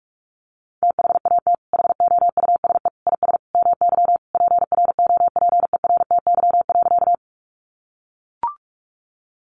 Morse code, heard as a keyed tone, about 44 wpm, 700 Hz. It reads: THAT HOUSE IS MY PROPERTY.
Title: Sentences from Top 1000 Words 45wpm